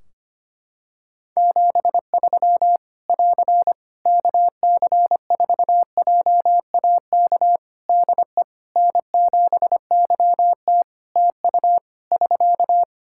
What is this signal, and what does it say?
Morse code, heard as a keyed tone, about 25 wpm, 700 Hz. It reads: 73 <AR> KC4JAK DE N7YT TU <SK>
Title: Practice QSO - Exchange 4 - Ending - 25wpm